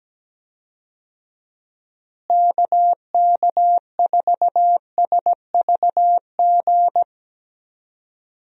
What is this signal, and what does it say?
Morse code, heard as a keyed tone, about 17 wpm, 700 Hz. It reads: KK4SVG
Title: USA Call Signs - 6 Characters - Repeated 3x - 17wpm